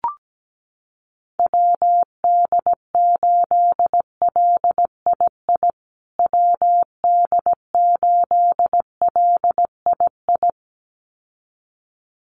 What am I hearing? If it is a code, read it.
Morse code, heard as a keyed tone, about 17 wpm, 700 Hz. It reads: WD8LII WD8LII
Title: USA Call Signs Repeated 2x 17wpm